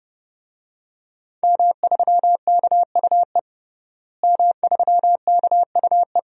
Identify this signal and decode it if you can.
Morse code, heard as a keyed tone, about 30 wpm, 700 Hz. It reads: M3KUE M3KUE
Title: DXCC-100 Call Signs - 5 Characters - Repeated 2x - 30wpm